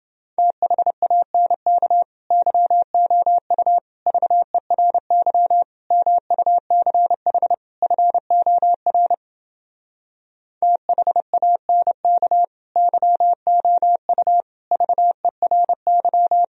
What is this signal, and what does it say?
Morse code, heard as a keyed tone, about 29 wpm, 700 Hz. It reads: THANK YOU VERY MUCH FOR THANK YOU VERY